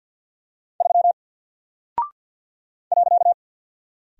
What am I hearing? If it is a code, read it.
Morse code, heard as a keyed tone, about 50 wpm, 700 Hz. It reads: 3 .
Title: Single Character (Letter:Number:Punctuation:Prosign) Alt 50wpm